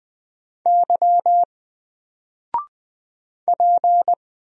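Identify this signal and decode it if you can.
Morse code, heard as a keyed tone, about 20 wpm, 700 Hz. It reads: Y P
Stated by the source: Single letter and number Morse drill at 20 wpm, from N4LCD.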